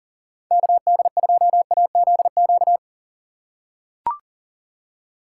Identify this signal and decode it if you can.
Morse code, heard as a keyed tone, about 37 wpm, 700 Hz. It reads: KD2AZQ
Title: USA Call Signs - 6 Characters - 4x Character Spacing - 40wpm